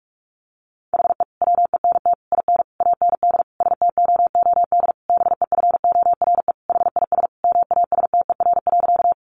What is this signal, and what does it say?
Morse code, heard as a keyed tone, about 44 wpm, 700 Hz. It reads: HE WENT IN AND STOOD BEFORE HIS MASTER.
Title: Sentences from Top 1300 Words 45wpm